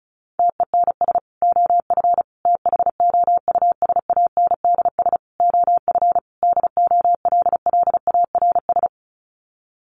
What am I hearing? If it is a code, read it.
Morse code, heard as a keyed tone, about 34 wpm, 700 Hz. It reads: TENS OF THOUSANDS OF DOLLARS
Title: Sets of 5 Words Encore2 35wpm